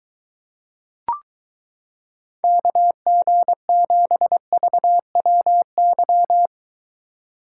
Morse code, heard as a keyed tone, about 23 wpm, 700 Hz. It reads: KG7VWY